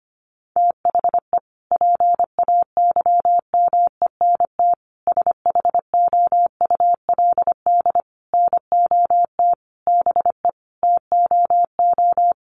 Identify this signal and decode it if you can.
Morse code, heard as a keyed tone, about 25 wpm, 700 Hz. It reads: THE PAYMENT SHOULD NOT BE TOO
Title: Sentences from Top 1200 Words 25wpm